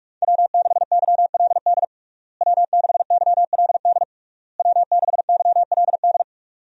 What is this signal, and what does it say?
Morse code, heard as a keyed tone, about 41 wpm, 700 Hz. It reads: W6YLD W6YLD W6YLD
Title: USA Call Signs - 5 Characters - Repeated 3x - 45wpm